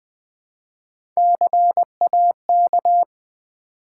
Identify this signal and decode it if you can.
Morse code, heard as a keyed tone, about 20 wpm, 700 Hz. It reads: CAK
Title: Build A Word - 1 Letter At A Time - 4-Letter Words - 20wpm